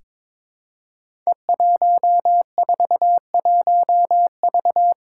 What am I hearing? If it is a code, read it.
Morse code, heard as a keyed tone, about 23 wpm, 700 Hz. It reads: E141V
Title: Five Challenging Letter-Number Combinations 22wpm